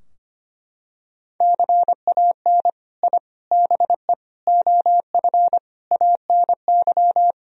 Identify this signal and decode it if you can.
Morse code, heard as a keyed tone, about 25 wpm, 700 Hz. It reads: CAN I BE OF ANY
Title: Sentences from Top 1000 Words - Review Part 4 - 25wpm